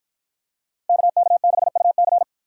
Morse code, heard as a keyed tone, about 50 wpm, 700 Hz. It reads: KK6UC